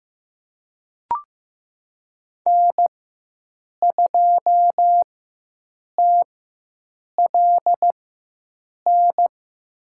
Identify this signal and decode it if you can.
Morse code, heard as a keyed tone, about 15 wpm, spaced as for 7 wpm, 700 Hz. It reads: N2TLN
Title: USA Call Signs - 5 Characters - 4x Character Spacing - 15wpm